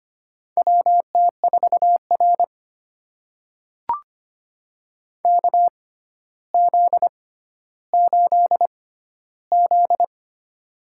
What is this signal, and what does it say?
Morse code, heard as a keyed tone, about 25 wpm, spaced as for 13 wpm, 700 Hz. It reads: WT4R KZ8Z
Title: USA Call Signs - 4 Characters - 6x Character Spacing - 25wpm